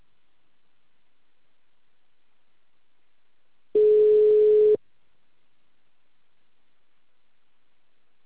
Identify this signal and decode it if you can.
Ringback tone (Europe, 425 Hz).